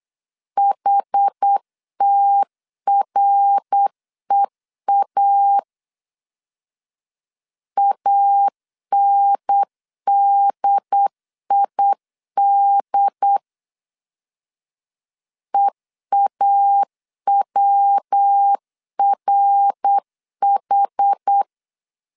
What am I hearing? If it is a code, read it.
Morse code, heard as a keyed tone, about 8 wpm, 800 Hz. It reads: HTREA ANDID EAWRH